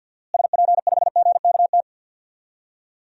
Morse code, heard as a keyed tone, about 51 wpm, 700 Hz. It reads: SP5GKT
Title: DXCC-100 Call Signs - 6 Characters - Repeated 3x - 50wpm